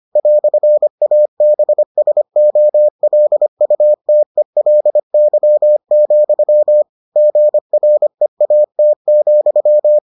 Morse code, heard as a keyed tone, about 25 wpm, 600 Hz. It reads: "ABSOLUTELY, GREAT,